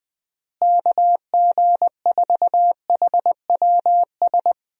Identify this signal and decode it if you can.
Morse code, heard as a keyed tone, about 20 wpm, 700 Hz. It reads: KG4HWS
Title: USA Call Signs - 6 Characters - Repeated 2x - 20wpm